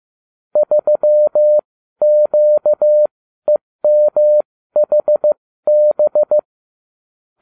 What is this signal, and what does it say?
Morse code, heard as a keyed tone, about 15 wpm, spaced as for 12 wpm, 600 Hz. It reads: 3QEMHB